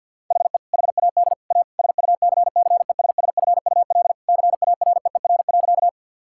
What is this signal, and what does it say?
Morse code, heard as a keyed tone, about 49 wpm, 700 Hz. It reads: HE HAD A SUCCESSFUL CAREER.
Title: Sentences from Top 1000 Words - Review Part 6 - 50wpm